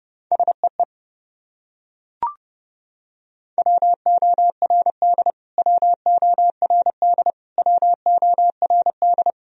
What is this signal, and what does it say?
Morse code, heard as a keyed tone, about 32 wpm, 700 Hz. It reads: SEE WORD WORD WORD
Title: Top 100 Words Repeated 30wpm